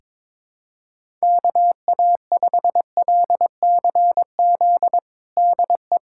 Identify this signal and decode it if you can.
Morse code, heard as a keyed tone, about 22 wpm, 700 Hz. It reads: KA5LCZ DE
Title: USA Callsign DE Callsign - 6 Characters - 22wpm